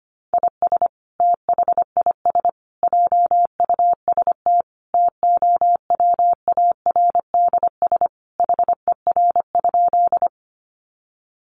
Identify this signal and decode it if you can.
Morse code, heard as a keyed tone, about 25 wpm, 700 Hz. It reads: IS THIS JUST TOWARDS HER?
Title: Sentences from Top 1100 Words 25wpm